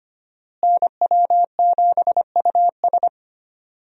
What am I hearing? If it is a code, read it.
Morse code, heard as a keyed tone, about 25 wpm, 700 Hz. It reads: NW7US
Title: USA Call Signs - 5 Characters - Repeated 3x - 25wpm